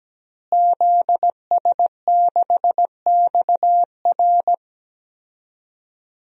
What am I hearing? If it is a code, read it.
Morse code, heard as a keyed tone, about 17 wpm, 700 Hz. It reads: ZS6XR
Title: DXCC-100 Call Signs - 5 Characters - Repeated 3x - 17wpm